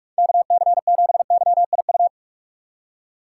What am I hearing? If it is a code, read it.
Morse code, heard as a keyed tone, about 42 wpm, 700 Hz. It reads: KC7YIU